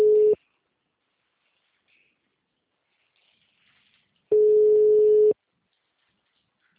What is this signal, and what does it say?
Ringback tone (Europe, 425 Hz).
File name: out- -2029- - - .18204